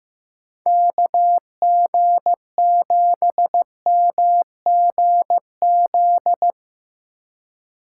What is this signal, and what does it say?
Morse code, heard as a keyed tone, about 15 wpm, 700 Hz. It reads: KG7MGZ
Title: USA Call Signs - 2x Character Spacing - 15wpm